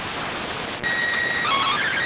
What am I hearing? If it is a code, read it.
SSTV (slow-scan television), PD 290.
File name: findings / CT AXIAL -572mm 2.gen-34 rx